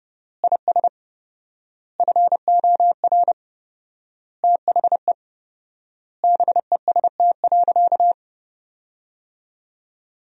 Morse code, heard as a keyed tone, about 30 wpm, 700 Hz. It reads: IS FOR THE BEST.
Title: 4x Word-Spacing - Sentences from Top 500 Words - 30wpm